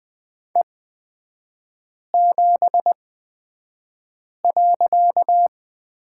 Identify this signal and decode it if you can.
Morse code, heard as a keyed tone, about 20 wpm, spaced as for 5 wpm, 700 Hz. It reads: E7.
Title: 255 Lesson 41e - Period - Three Characters 20wpm